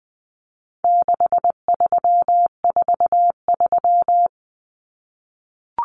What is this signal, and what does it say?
Morse code, heard as a keyed tone, about 20 wpm, 700 Hz. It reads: 6343
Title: Common Cut Numbers - 4-Digit 20wpm